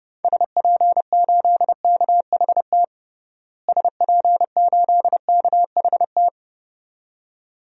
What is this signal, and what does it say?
Morse code, heard as a keyed tone, about 30 wpm, 700 Hz. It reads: SP8KHT SP8KHT
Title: DXCC-100 Call Signs - 6 Characters - Repeated 2x - 30wpm